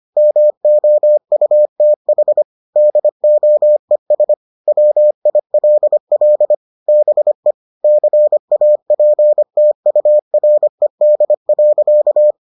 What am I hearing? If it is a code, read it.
Morse code, heard as a keyed tone, about 25 wpm, 600 Hz. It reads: MOUTH DOES WILL BE CAPTURED.